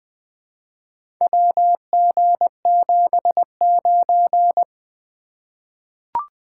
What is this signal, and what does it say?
Morse code, heard as a keyed tone, about 20 wpm, 700 Hz. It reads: WG79
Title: Four Letter-Number Combinations - 20wpm